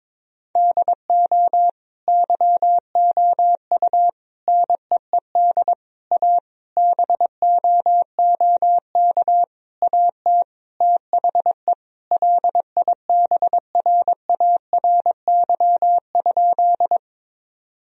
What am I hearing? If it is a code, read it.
Morse code, heard as a keyed tone, about 22 wpm, 700 Hz. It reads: DO YOU NEED A BOOK AT THE LIBRARY?
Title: Sentences from Top 1000 Words - Review Part 4 - 22wpm